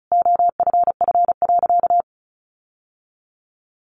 Morse code, heard as a keyed tone, about 34 wpm, 700 Hz. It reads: OFF.
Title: Sentences from Top 1100 Words 35wpm